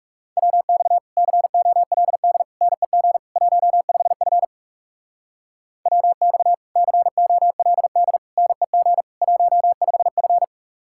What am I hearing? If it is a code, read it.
Morse code, heard as a keyed tone, about 37 wpm, 700 Hz. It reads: WX COLD NEG 15F WX COLD NEG 15F
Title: Speed-Racing Common QSO Phrases - 30wpm